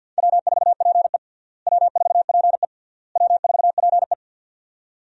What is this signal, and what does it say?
Morse code, heard as a keyed tone, about 52 wpm, 700 Hz. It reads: W4PE W4PE W4PE
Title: USA Call Signs - 4 Characters - Repeated 3x - 50wpm